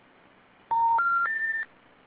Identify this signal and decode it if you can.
Special information tone.